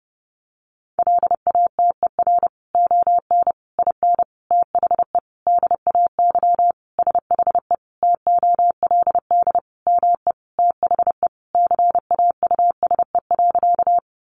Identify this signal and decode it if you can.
Morse code, heard as a keyed tone, about 30 wpm, 700 Hz. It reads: LATER ON IN THE DAY SHE TOLD ME THE CAUSE.
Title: Sentences from Top 700 Words 30wpm